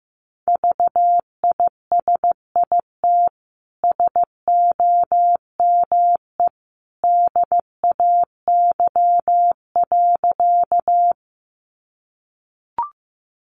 Morse code, heard as a keyed tone, about 15 wpm, 700 Hz. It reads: VISIT SOME DAY.
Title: Sentences from Top 900 Words 15wpm